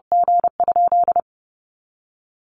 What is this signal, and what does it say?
Morse code, heard as a keyed tone, about 28 wpm, 700 Hz. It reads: G?
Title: Sentences from Top 400 Words 30wpm